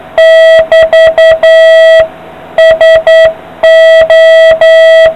Morse code, heard as a keyed tone, about 9 wpm, 660 Hz, with noise. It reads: <BT> SO